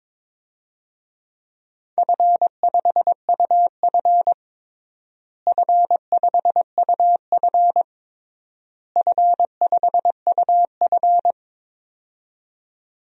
Morse code, heard as a keyed tone, about 22 wpm, 700 Hz. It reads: F5UF F5UF F5UF
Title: DXCC-100 Call Signs - 4 Characters - Repeated 3x - 22wpm